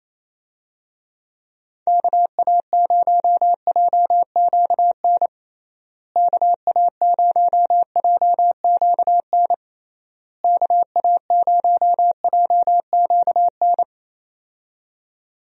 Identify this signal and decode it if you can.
Morse code, heard as a keyed tone, about 28 wpm, 700 Hz. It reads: KA0JQN KA0JQN KA0JQN
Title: USA Call Signs - 6 Characters - Repeated 3x - 28wpm